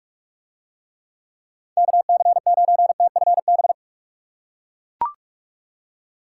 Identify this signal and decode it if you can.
Morse code, heard as a keyed tone, about 44 wpm, 700 Hz. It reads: KC9TFB